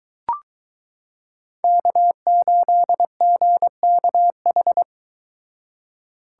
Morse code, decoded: K8GKH

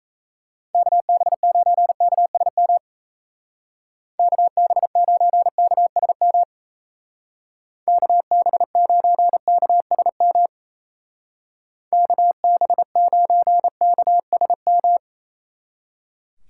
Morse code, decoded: KB9KSM KB9KSM KB9KSM KB9KSM

